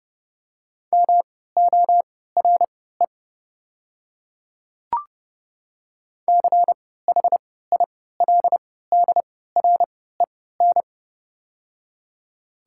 Morse code, heard as a keyed tone, about 32 wpm, spaced as for 17 wpm, 700 Hz. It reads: MORE CHILDREN
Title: 3x Character Spacing - Top 300 Words - 30wpm